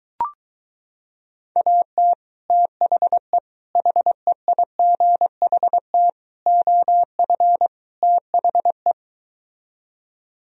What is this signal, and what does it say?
Morse code, heard as a keyed tone, about 23 wpm, 700 Hz. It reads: AT THE HEIGHT OF THE